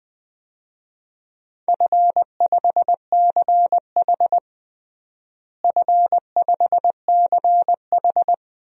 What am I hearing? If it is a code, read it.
Morse code, heard as a keyed tone, about 21 wpm, 700 Hz. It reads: F5CH F5CH
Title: DXCC-100 Call Signs - 4 Characters - Repeated 3x - 20wpm